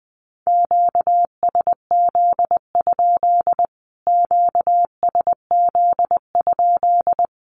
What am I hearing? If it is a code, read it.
Morse code, heard as a keyed tone, about 20 wpm, 700 Hz. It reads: QSZ? QSZ?
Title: Q-Codes Repeated 3x 20wpm